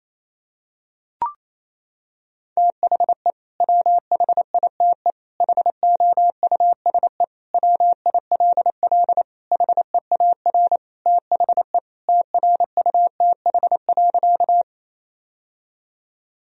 Morse code, heard as a keyed tone, about 28 wpm, 700 Hz. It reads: THE WHITE HOUSE WILL HEAR THE TRUTH.
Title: Sentences from Top 1000 Words - Review Part 1 - 28wpm